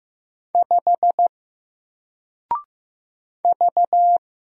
Morse code, heard as a keyed tone, about 15 wpm, 700 Hz. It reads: 5 V
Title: Single Character (Letter:Number:Punctuation:Prosign) 15wpm